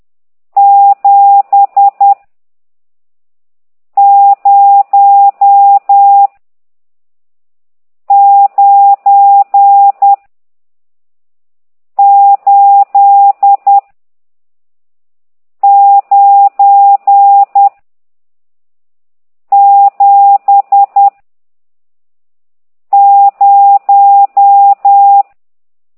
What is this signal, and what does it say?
Morse code, heard as a keyed tone, about 10 wpm, 800 Hz. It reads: 7 0 9 8 9 7 0
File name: ct2idw / exerc14